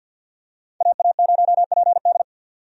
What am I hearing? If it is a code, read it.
Morse code, heard as a keyed tone, about 46 wpm, 700 Hz. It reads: AA0PD
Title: USA Call Signs - 5 Characters - 50wpm